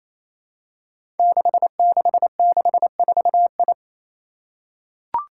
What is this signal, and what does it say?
Morse code, heard as a keyed tone, about 28 wpm, 700 Hz. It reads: 6664I
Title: Five Challenging Letter-Number Combinations 28wpm